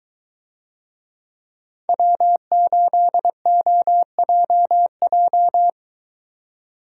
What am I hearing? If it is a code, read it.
Morse code, heard as a keyed tone, about 23 wpm, 700 Hz. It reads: W8OJJ